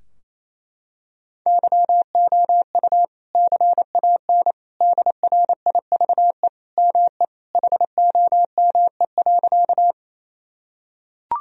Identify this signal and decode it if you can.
Morse code, heard as a keyed tone, about 28 wpm, 700 Hz. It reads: YOU CAN DRIVE ME HOME.
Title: Sentences from Top 1000 Words - Review Part 2 - 28wpm